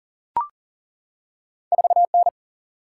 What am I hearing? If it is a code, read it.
Morse code, heard as a keyed tone, about 39 wpm, 700 Hz. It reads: 4 N
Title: Single-Letter Single-Number 40wpm